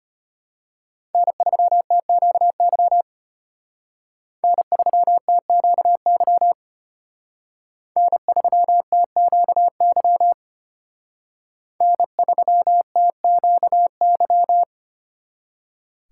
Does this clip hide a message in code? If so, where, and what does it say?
Yes: Morse code from 1.1 to 14.6 s, heard as a keyed tone, about 32 wpm, 700 Hz. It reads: N3TQY N3TQY N3TQY N3TQY